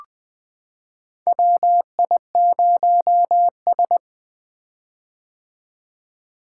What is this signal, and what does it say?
Morse code, heard as a keyed tone, about 20 wpm, 700 Hz. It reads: WI0S
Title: 2023 POTA Activator Call Signs - 20wpm